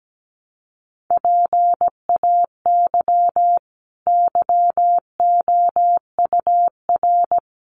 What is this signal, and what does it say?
Morse code, heard as a keyed tone, about 17 wpm, 700 Hz. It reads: PAY YOUR
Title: Sentences from Top 900 Words 17wpm